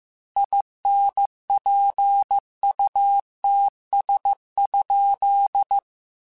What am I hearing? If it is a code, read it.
Morse code, heard as a keyed tone, about 15 wpm, 800 Hz. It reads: INPUTS?
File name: Technical Subjects / Technical Subjects-mp3s / AND Gate @13